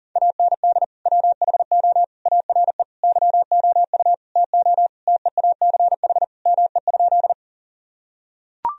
Morse code, heard as a keyed tone, about 40 wpm, 700 Hz. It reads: AND WHO ARE YOU TO TEACH ME?
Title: Sentences from Top 1000 Words - Review Part 2 - 40wpm